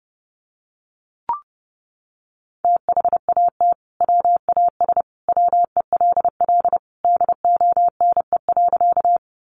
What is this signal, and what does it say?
Morse code, heard as a keyed tone, about 28 wpm, 700 Hz. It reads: THAT WAS WELL DONE.
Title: Sentences from Top 500 Words 30wpm